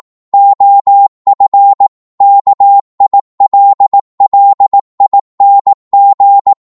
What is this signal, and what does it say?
Morse code, heard as a keyed tone, about 18 wpm, 800 Hz. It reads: OF KILLING